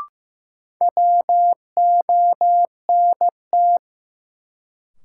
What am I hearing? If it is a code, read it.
Morse code, heard as a keyed tone, about 15 wpm, 700 Hz. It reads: WONT
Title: Sentences from Top 1000 Words - Review Part 6 - New Word Drill - 15wpm